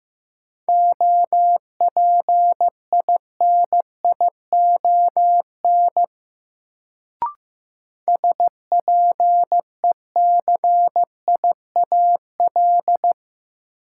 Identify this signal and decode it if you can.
Morse code, heard as a keyed tone, about 15 wpm, 700 Hz. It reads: OPINION SPECIAL
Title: Sentences from Top 1200 Words - New Word Drill Alt - 15wpm